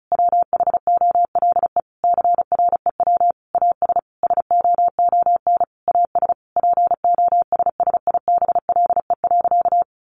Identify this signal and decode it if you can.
Morse code, heard as a keyed tone, about 34 wpm, 700 Hz. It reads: WHOLE CREW AS SOON AS POSSIBLE.
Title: Sentences from Top 1500 Words 35wpm